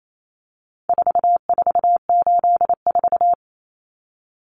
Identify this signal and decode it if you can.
Morse code, heard as a keyed tone, about 28 wpm, 700 Hz. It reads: 4484
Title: Common Cut Numbers - 4-Digit 28wpm